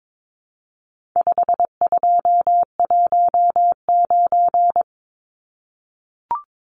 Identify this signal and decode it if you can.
Morse code, heard as a keyed tone, about 22 wpm, 700 Hz. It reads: 5219
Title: Numbers - 4-Digit 22wpm